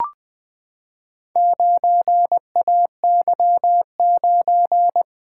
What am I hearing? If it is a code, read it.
Morse code, heard as a keyed tone, about 21 wpm, 700 Hz. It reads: 9AY9
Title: Four Letter-Number Combinations - 20wpm